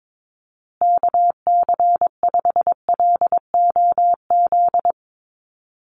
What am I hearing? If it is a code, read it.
Morse code, heard as a keyed tone, about 22 wpm, 700 Hz. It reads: KC5LOZ
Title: USA Call Signs Repeated 2x 22wpm